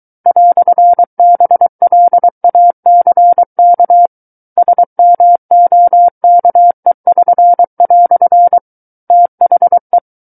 Morse code, heard as a keyed tone, about 23 wpm, 700 Hz. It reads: "BLACK SMOKE<SN>" THE
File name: Book One / 14-Chapter 14 - 23 WPM